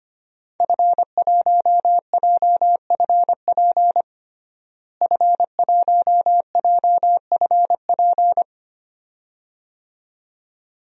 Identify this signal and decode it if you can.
Morse code, heard as a keyed tone, about 25 wpm, 700 Hz. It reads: F1JFP F1JFP
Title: DXCC-100 Call Signs - 5 Characters - Repeated 2x - 25wpm